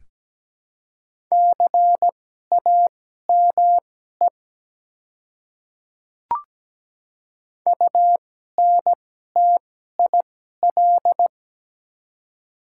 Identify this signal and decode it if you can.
Morse code, heard as a keyed tone, about 17 wpm, spaced as for 12 wpm, 700 Hz. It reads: CAME UNTIL
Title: 2x Character Spacing - Top 300 Words - 17wpm